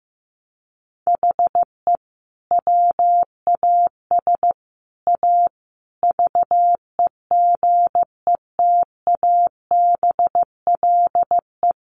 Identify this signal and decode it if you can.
Morse code, heard as a keyed tone, about 15 wpm, 700 Hz. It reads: HE WAS A VEGETABLE
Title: Sentences from Top 1300 Words 15wpm